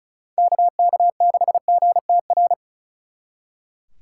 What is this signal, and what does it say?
Morse code, heard as a keyed tone, about 45 wpm, 700 Hz. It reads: KK6GTR